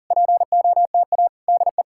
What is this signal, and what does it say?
Morse code, heard as a keyed tone, about 46 wpm, 700 Hz. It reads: POTA DE